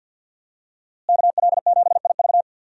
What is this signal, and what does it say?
Morse code, heard as a keyed tone, about 53 wpm, 700 Hz. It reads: KL7IV
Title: USA Call Signs - 5 Characters - Repeated 3x - 50wpm